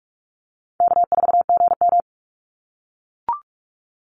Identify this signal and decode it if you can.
Morse code, heard as a keyed tone, about 44 wpm, 700 Hz. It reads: K4GM
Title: Call Signs - 45wpm